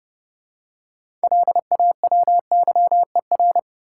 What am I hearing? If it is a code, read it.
Morse code, heard as a keyed tone, about 30 wpm, 700 Hz. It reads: LAWYER
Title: Six Letter Words - 30wpm